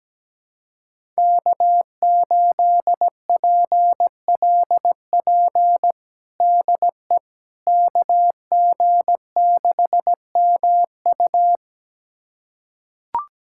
Morse code, heard as a keyed tone, about 17 wpm, 700 Hz. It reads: K8PLP DE KG6MU
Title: USA Callsign DE Callsign - 5 Characters - 17wpm